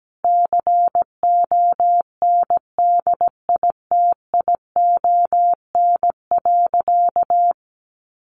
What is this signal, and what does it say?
Morse code, heard as a keyed tone, about 17 wpm, 700 Hz. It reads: CONDITION.
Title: Sentences from Top 1100 Words 17wpm